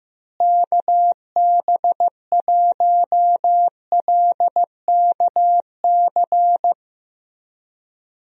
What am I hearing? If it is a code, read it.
Morse code, heard as a keyed tone, about 15 wpm, 700 Hz. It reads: KB1LKC
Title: USA Call Signs - 6 Characters - Repeated 2x - 15wpm